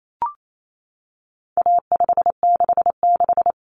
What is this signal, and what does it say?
Morse code, heard as a keyed tone, about 28 wpm, 700 Hz. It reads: A566
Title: Common Cut Numbers - 4-Digit 28wpm